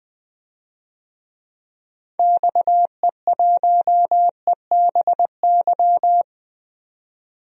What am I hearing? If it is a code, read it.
Morse code, heard as a keyed tone, about 20 wpm, 700 Hz. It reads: XE1EBY
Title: DXCC-100 Call Signs - 6 Characters - 20wpm